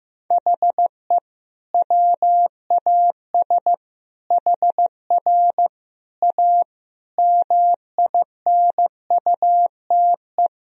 Morse code, heard as a keyed tone, about 14 wpm, 700 Hz. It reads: HE WAS HR A MINUTE
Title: Sentences from Top 500 Words and Abbreviations 15wpm